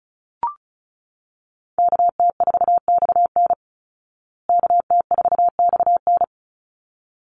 Morse code, heard as a keyed tone, about 34 wpm, 700 Hz. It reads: KT4XN KT4XN
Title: Call Signs Repeated 3x - 35wpm